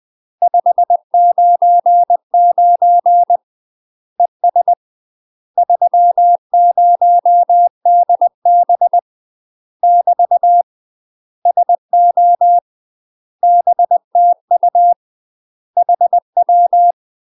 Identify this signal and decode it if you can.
Morse code, heard as a keyed tone, about 20 wpm, 700 Hz. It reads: 599 ES 30DB <BT> SO BTU HW
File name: UK MASTER QSOs TRAINING / 20WPM / UK MASTER QSOs 0029